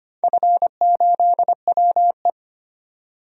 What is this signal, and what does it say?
Morse code, heard as a keyed tone, about 25 wpm, 700 Hz. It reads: F8WE
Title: DXCC-100 Call Signs - 4 Characters - 25wpm